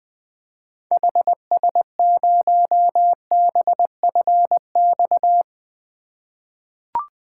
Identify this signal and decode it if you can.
Morse code, heard as a keyed tone, about 20 wpm, 700 Hz. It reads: HS0BFX
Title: 2023 POTA Activator Call Signs - 20wpm